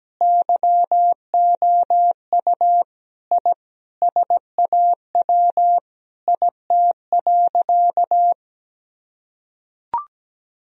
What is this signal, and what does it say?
Morse code, heard as a keyed tone, about 17 wpm, 700 Hz. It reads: YOU I SAW IT.
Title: Sentences from Top 1000 Words - Review Part 4 - 17wpm